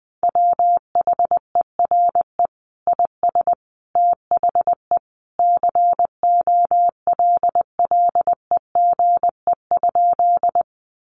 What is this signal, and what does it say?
Morse code, heard as a keyed tone, about 20 wpm, 700 Hz. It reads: WHERE IS THE COLLEGE?
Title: Sentences from Top 1300 Words 20wpm